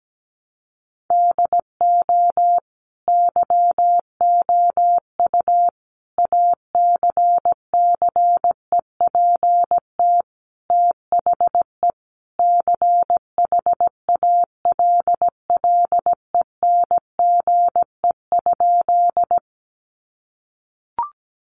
Morse code, heard as a keyed tone, about 17 wpm, 700 Hz. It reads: DO YOU ACCEPT THE CHALLENGE?
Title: Sentences from Top 1300 Words 17wpm